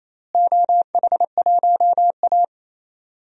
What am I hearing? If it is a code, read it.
Morse code, heard as a keyed tone, about 28 wpm, 700 Hz. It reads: OH1A
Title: DXCC-100 Call Signs - 4 Characters - Repeated 3x - 28wpm